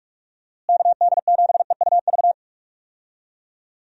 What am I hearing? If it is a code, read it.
Morse code, heard as a keyed tone, about 46 wpm, 700 Hz. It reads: KD7EUV
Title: USA Call Signs - 6 Characters - Repeated 3x - 45wpm